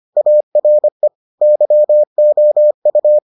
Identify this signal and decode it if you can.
Morse code, heard as a keyed tone, about 25 wpm, 600 Hz. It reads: ARE YOU